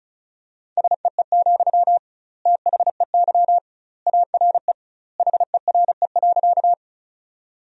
Morse code, heard as a keyed tone, about 36 wpm, 700 Hz. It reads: SEE, THEY ARE HERE.
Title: 2x Word-Spacing - Sentences from Top 200 Words - 35wpm